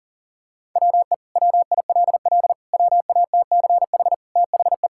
Morse code, heard as a keyed tone, about 40 wpm, 700 Hz. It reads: WE WILL WATCH THE
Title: Sentences from Top 1000 Words - Review Part 5 - 40wpm